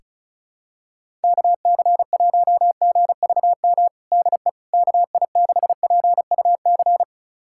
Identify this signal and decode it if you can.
Morse code, heard as a keyed tone, about 36 wpm, 700 Hz. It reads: KC1GVM DE KI6PUC
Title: USA Callsign DE Callsign 35wpm